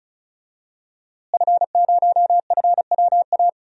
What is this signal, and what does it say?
Morse code, heard as a keyed tone, about 36 wpm, 700 Hz. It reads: F0FWA
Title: DXCC-100 Call Signs - 5 Characters - Repeated 2x - 35wpm